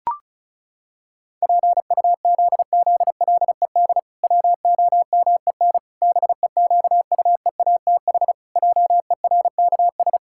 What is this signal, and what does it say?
Morse code, heard as a keyed tone, about 34 wpm, 700 Hz. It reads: PUZZLED WOMEN BEQUEATH JERKS
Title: Pangram Challenge v2 - 35wpm